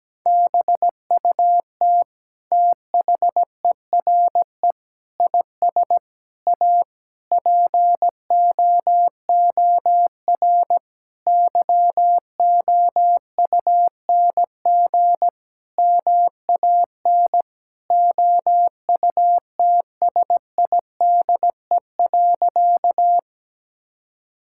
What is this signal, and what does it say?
Morse code, heard as a keyed tone, about 17 wpm, 700 Hz. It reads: BUT THERE IS A POOR YOUNG MAN OUTSIDE.
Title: Sentences from Top 1000 Words - Review Part 6 - 17wpm